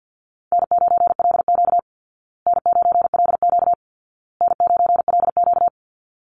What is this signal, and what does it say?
Morse code, heard as a keyed tone, about 50 wpm, 700 Hz. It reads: N9LQ N9LQ N9LQ
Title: Call Signs Repeated 2x - 50wpm